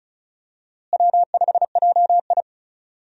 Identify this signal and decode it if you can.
Morse code, heard as a keyed tone, about 36 wpm, 700 Hz. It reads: W5JI
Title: USA Call Signs - 4 Characters - Repeated 3x - 35wpm